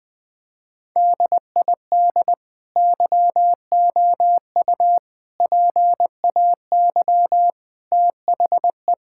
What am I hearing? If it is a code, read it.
Morse code, heard as a keyed tone, about 20 wpm, 700 Hz. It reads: DID YOU PAY THE
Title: Sentences from Top 1000 Words - Review Part 5 - 20wpm